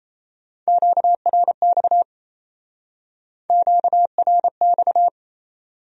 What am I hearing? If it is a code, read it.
Morse code, heard as a keyed tone, about 33 wpm, 700 Hz. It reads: QRX QRX
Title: Speed-Racing Q-Codes - 28wpm